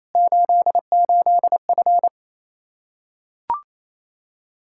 Morse code, heard as a keyed tone, about 28 wpm, 700 Hz. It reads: 88F
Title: K1USN SST QSO Practice 28wpm